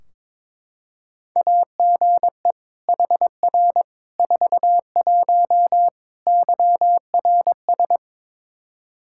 Morse code, heard as a keyed tone, about 22 wpm, 700 Hz. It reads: AGE HR 41 YRS
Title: Practice QSO - Exchange 3 - More Info - 22wpm